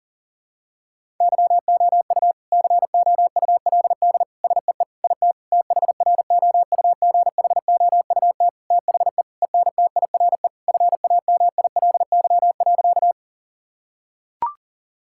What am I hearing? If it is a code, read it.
Morse code, heard as a keyed tone, about 37 wpm, 700 Hz. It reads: YOU COULD SEE IT THROUGHOUT THE ENTIRE FAMILY.
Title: Sentences from Top 1000 Words - Review Part 5 - 40wpm